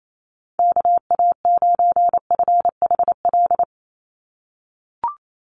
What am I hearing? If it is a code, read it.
Morse code, heard as a keyed tone, about 28 wpm, 700 Hz. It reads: KA9FHL
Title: USA Call Signs 28wpm